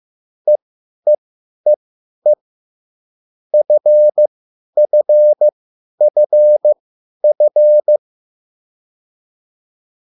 Morse code, heard as a keyed tone, about 15 wpm, spaced as for 10 wpm, 600 Hz. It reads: EEEE FFFF
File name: cwbook